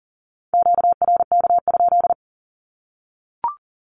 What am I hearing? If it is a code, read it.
Morse code, heard as a keyed tone, about 40 wpm, 700 Hz. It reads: QRK?
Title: Q-Codes 40wpm